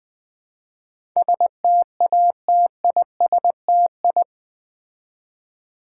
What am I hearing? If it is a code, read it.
Morse code, heard as a keyed tone, about 20 wpm, 700 Hz. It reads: STATISTI
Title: Build A Word - 4 Letters At A Time - 11-Letter Words - 20wpm